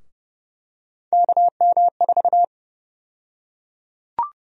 Morse code, heard as a keyed tone, about 31 wpm, 700 Hz. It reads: KM4
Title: USA Call Sign Prefixes - Repeated 2x - 30wpm